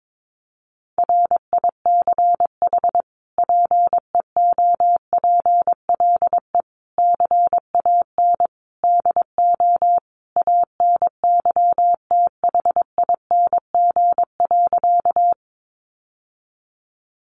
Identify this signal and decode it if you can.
Morse code, heard as a keyed tone, about 22 wpm, 700 Hz. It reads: RICH PEOPLE CAN DO ANYTHING.
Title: Sentences from Top 1100 Words 22wpm